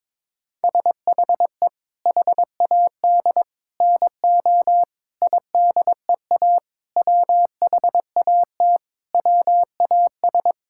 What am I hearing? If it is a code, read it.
Morse code, heard as a keyed tone, about 22 wpm, 700 Hz. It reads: SHE HAD NO IDEA WHAT WAS
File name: Sentences with 7-Word Phrases - 22wpm